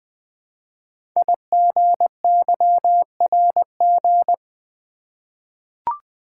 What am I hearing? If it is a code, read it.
Morse code, heard as a keyed tone, about 20 wpm, 700 Hz. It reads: IGYRG
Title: Code Groups - 5 Letters - 20wpm